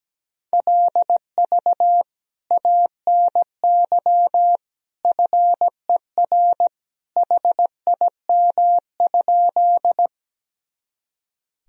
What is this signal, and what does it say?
Morse code, heard as a keyed tone, about 17 wpm, 700 Hz. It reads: LV ANY FER HIM?
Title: Sentences from Top 400 Words and Abbreviations 17wpm